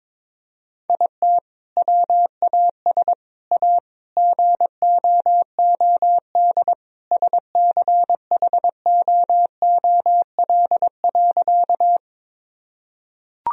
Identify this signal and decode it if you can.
Morse code, heard as a keyed tone, about 23 wpm, 700 Hz. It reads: IT WAS A GOOD SCHOOL.
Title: Sentences from Top 1000 Words - Review Part 1 - 22wpm